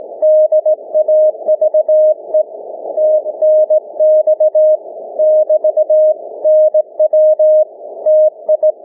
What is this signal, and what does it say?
Morse code, heard as a keyed tone, about 20 wpm, 620 Hz. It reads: DAVE TNX <BT> NW TI
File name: cw / ssn 30M